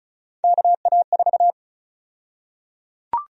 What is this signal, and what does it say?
Morse code, heard as a keyed tone, about 36 wpm, 700 Hz. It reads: KA4
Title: USA Call Sign Prefixes - Repeated 2x - 35wpm